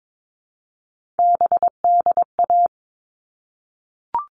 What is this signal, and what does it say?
Morse code, heard as a keyed tone, about 22 wpm, 700 Hz. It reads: BDA
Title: All Cut Numbers - 3-Digit 22wpm